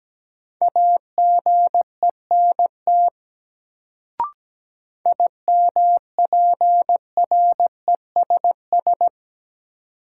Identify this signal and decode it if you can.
Morse code, heard as a keyed tone, about 17 wpm, 700 Hz. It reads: AGENT IMPRESS
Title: Sentences from Top 1600 Words - New Word Drill - 17wpm